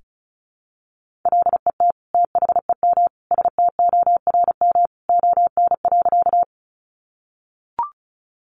Morse code, heard as a keyed tone, about 34 wpm, 700 Hz. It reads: LET THEM STORM ON.
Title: Sentences from Top 1500 Words 35wpm